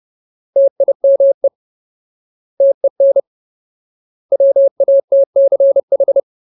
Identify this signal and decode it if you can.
Morse code, heard as a keyed tone, about 30 wpm, 550 Hz. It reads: TIME TEN WATCH
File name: Ultimate Word List / 30WPM Ultimate